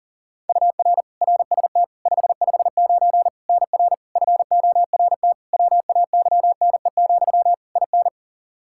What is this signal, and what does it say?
Morse code, heard as a keyed tone, about 40 wpm, 700 Hz. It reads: UR RST 559 NR FORT WAYNE, IN
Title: Practice QSO - Exchange 1 - Beginning - 40wpm